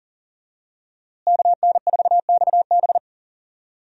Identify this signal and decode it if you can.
Morse code, heard as a keyed tone, about 40 wpm, 700 Hz. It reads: KN4XB
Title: USA Call Signs - 5 Characters - Repeated 3x - 40wpm